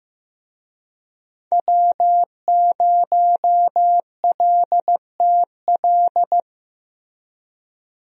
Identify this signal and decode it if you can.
Morse code, heard as a keyed tone, about 15 wpm, 700 Hz. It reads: W0LTL
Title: USA Call Signs - 5 Characters - Repeated 3x - 15wpm